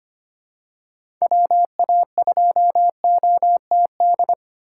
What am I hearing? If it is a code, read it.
Morse code, heard as a keyed tone, about 25 wpm, 700 Hz. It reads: WA2OTD
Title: USA Call Signs - 6 Characters - 3x Character Spacing - 25wpm